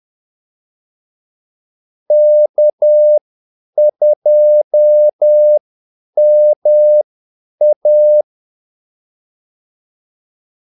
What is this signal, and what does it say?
Morse code, heard as a keyed tone, about 10 wpm, spaced as for 8 wpm, 600 Hz. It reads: K2MA
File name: Callsigns-10-8wpm